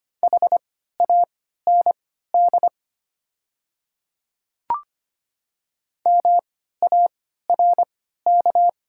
Morse code, heard as a keyed tone, about 25 wpm, spaced as for 14 wpm, 700 Hz. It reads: HAND MARK
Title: 3x Character Spacing - Top 300 Words - 25wpm